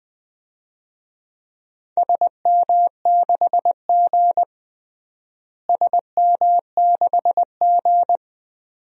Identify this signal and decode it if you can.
Morse code, heard as a keyed tone, about 21 wpm, 700 Hz. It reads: SM6G SM6G
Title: DXCC-100 Call Signs - 4 Characters - Repeated 2x - 20wpm